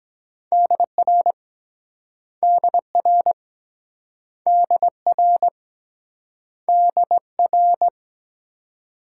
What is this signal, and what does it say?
Morse code, heard as a keyed tone, about 23 wpm, 700 Hz. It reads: DR DR DR DR